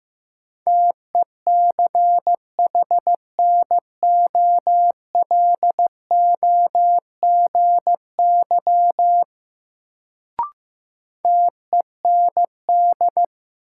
Morse code, heard as a keyed tone, about 15 wpm, 700 Hz. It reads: TECHNOLOGY TEND
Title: Sentences from Top 1000 Words - Review Part 5 - New Word Drill - 15wpm